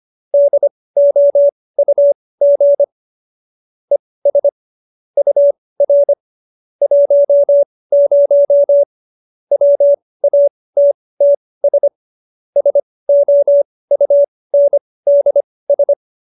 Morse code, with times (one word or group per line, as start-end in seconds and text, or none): 0.3-2.8 DOUG
3.9-4.5 ES
5.2-6.1 UR
6.8-8.8 10
9.5-11.9 WATTS
12.6-15.9 SOUNDS